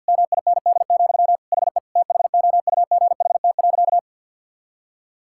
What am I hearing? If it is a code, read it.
Morse code, heard as a keyed tone, about 50 wpm, 700 Hz. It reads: MIND, HE THOUGHT.
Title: Sentences from Top 1000 Words - Review Part 6 - 50wpm